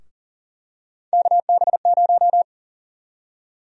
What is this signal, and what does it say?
Morse code, heard as a keyed tone, about 40 wpm, 700 Hz. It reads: KB0
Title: USA Call Sign Prefixes - 40wpm